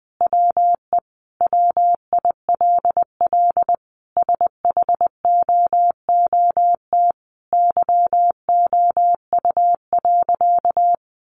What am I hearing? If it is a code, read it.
Morse code, heard as a keyed tone, about 20 wpm, 700 Hz. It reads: WE WILL SHOOT YOU.
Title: Sentences from Top 1200 Words 20wpm